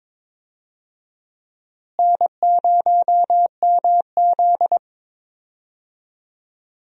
Morse code, heard as a keyed tone, about 22 wpm, 700 Hz. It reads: N0MZ
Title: USA Call Signs - 4 Characters - Repeated 2x - 22wpm